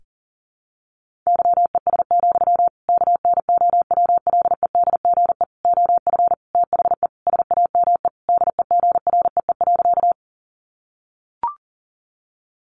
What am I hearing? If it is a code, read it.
Morse code, heard as a keyed tone, about 40 wpm, 700 Hz. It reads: YES, KNOWLEDGE OF THE SAME DEGREE.
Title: Sentences from Top 1500 Words 40wpm